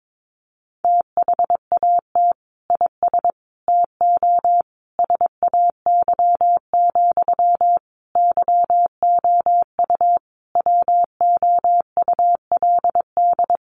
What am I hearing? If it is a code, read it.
Morse code, heard as a keyed tone, about 22 wpm, 700 Hz. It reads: THAT IS TO SAY, YOU WOULD
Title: Sentences from Top 700 Words 22wpm